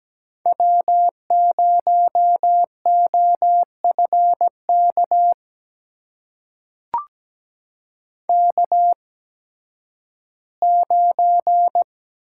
Morse code, heard as a keyed tone, about 17 wpm, spaced as for 11 wpm, 700 Hz. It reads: W0OFK K9
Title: USA Call Signs - 5 Characters - 8x Character Spacing - 17wpm